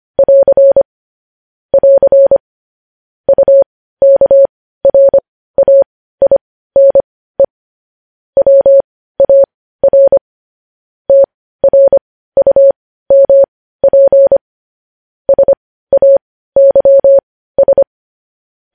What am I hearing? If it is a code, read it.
Morse code, heard as a keyed tone, about 25 wpm, spaced as for 15 wpm, 550 Hz. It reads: <AR> <AR> UKRAINE WAR TRUMP SAYS
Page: news headlines in morse code at 15 wpm